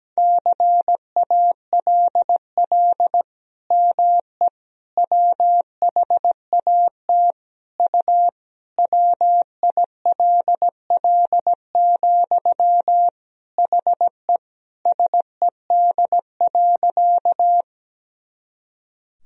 Morse code, heard as a keyed tone, about 17 wpm, 700 Hz. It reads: CALL ME WHAT U WILL, HE SED.
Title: Sentences from Top 200 Words and Abbreviations 17wpm